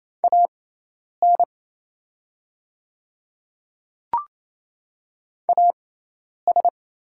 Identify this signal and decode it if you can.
Morse code, heard as a keyed tone, about 27 wpm, spaced as for 10 wpm, 690 Hz. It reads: AN AS
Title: 6x Character Spacing - Top 100 Words - 28wpm